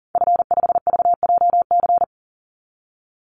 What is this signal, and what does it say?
Morse code, heard as a keyed tone, about 40 wpm, 700 Hz. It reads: F5VJC
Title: Call Signs - 40wpm